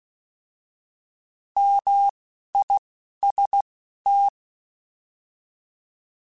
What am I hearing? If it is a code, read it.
Morse code, heard as a keyed tone, about 16 wpm, spaced as for 11 wpm, 800 Hz. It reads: MIST
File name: Alphabetical / Alphabetical mp3s / Ms @10